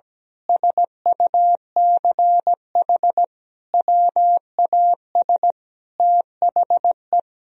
Morse code, heard as a keyed tone, about 17 wpm, 700 Hz. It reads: SUCH WAS THE
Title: Sentences from Top 1000 Words - Review Part 3 - 17wpm